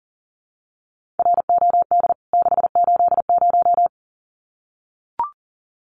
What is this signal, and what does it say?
Morse code, heard as a keyed tone, about 40 wpm, 700 Hz. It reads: ROD 680
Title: CWT Exchange 40wpm